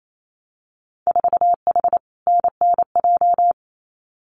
Morse code, heard as a keyed tone, about 28 wpm, 700 Hz. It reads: 4H NNJ
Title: Winter Field Day Exchanges Encore 28wpm